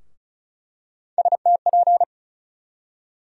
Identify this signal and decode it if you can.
Morse code, heard as a keyed tone, about 34 wpm, 700 Hz. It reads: STP
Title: USA Call Sign Postfixes - 3 Letters 35wpm